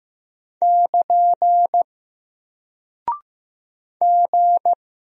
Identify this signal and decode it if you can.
Morse code, heard as a keyed tone, about 15 wpm, 700 Hz. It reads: <KN> G
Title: Single Character (Letter:Number:Punctuation:Prosign) Alt 15wpm